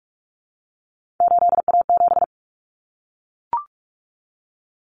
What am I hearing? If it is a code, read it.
Morse code, heard as a keyed tone, about 44 wpm, 700 Hz. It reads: 8A7
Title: Common Cut Numbers - 3-Digit 45wpm